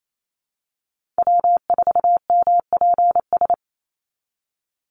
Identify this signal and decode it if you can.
Morse code, heard as a keyed tone, about 28 wpm, 700 Hz. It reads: W4MPS
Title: NA SOTA Chaser Callsigns 28wpm